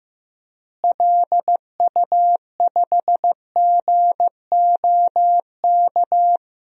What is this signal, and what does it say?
Morse code, heard as a keyed tone, about 14 wpm, 700 Hz. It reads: LU5GOK
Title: DXCC-100 Call Signs - 6 Characters - Repeated 3x - 15wpm